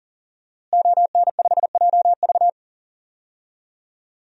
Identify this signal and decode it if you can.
Morse code, heard as a keyed tone, about 40 wpm, 700 Hz. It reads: ON5JV
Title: DXCC-100 Call Signs - 5 Characters - Repeated 2x - 40wpm